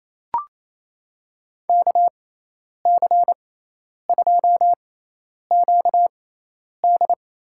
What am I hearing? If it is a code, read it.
Morse code, heard as a keyed tone, about 27 wpm, spaced as for 10 wpm, 700 Hz. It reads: KC2QD
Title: USA Call Signs - 6 Characters - 6x Character Spacing - 28wpm